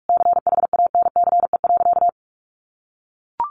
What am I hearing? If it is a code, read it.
Morse code, heard as a keyed tone, about 44 wpm, 700 Hz. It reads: CHANCE.
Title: Sentences from Top 1400 Words 45wpm